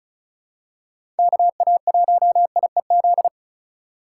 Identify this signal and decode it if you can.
Morse code, heard as a keyed tone, about 36 wpm, 700 Hz. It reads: KA1IEZ
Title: USA Call Signs - 6 Characters - 3x Character Spacing - 35wpm